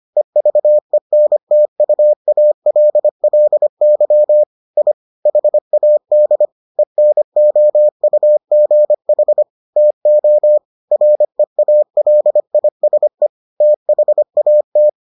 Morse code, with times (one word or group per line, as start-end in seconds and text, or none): 0.2-4.4 EVENTUALLY
4.8-4.9 I
5.3-6.5 HAD
6.8-9.4 ENOUGH
9.8-10.6 TO
10.9-13.3 REALISE
13.6-14.9 THAT